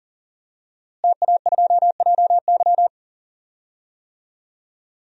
Morse code, heard as a keyed tone, about 40 wpm, 700 Hz. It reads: TA2JY
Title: DXCC-100 Call Signs - 5 Characters - Repeated 2x - 40wpm